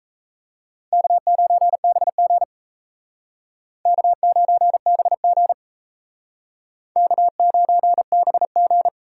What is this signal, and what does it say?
Morse code, heard as a keyed tone, about 35 wpm, 700 Hz. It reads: K9BG K9BG K9BG
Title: Speed Racing - USA Call Signs 28wpm